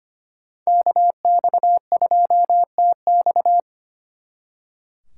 Morse code, heard as a keyed tone, about 33 wpm, 700 Hz. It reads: KX2TX